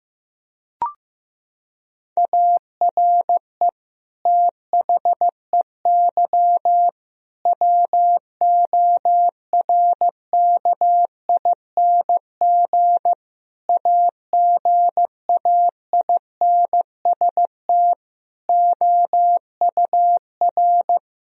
Morse code, heard as a keyed tone, about 15 wpm, 700 Hz. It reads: ARE THEY WORKING AGAINST OUR
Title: Sentences from Top 1000 Words - Review Part 4 - 15wpm